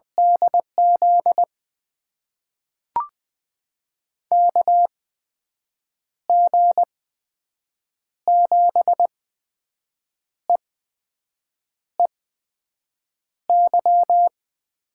Morse code, heard as a keyed tone, about 20 wpm, spaced as for 6 wpm, 700 Hz. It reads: DZ KG7EEY
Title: USA Call Signs - 6 Characters - 8x Character Spacing - 20wpm